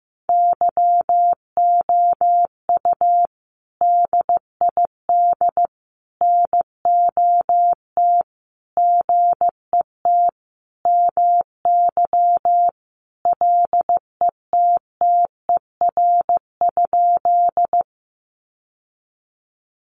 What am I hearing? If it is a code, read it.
Morse code, heard as a keyed tone, about 15 wpm, 700 Hz. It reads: YOU DID NOT GET MY LETTER?
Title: Sentences from Top 600 Words 15wpm